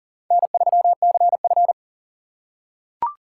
Morse code, heard as a keyed tone, about 40 wpm, 700 Hz. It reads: N3CF
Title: USA Call Signs - 4 Characters - 6x Character Spacing - 40wpm